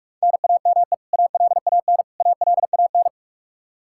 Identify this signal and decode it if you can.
Morse code, heard as a keyed tone, about 41 wpm, 700 Hz. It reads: NAME ALAN ALAN
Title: Practice QSO - Exchange 1 - Beginning - 45wpm